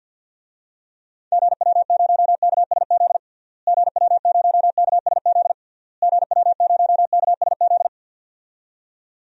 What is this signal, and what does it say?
Morse code, heard as a keyed tone, about 50 wpm, 700 Hz. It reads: GW0KSZ GW0KSZ GW0KSZ
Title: DXCC-100 Call Signs - 6 Characters - Repeated 3x - 50wpm